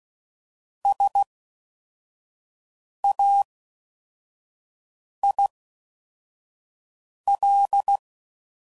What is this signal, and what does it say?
Morse code, heard as a keyed tone, about 16 wpm, spaced as for 4 wpm, 800 Hz. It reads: SAIL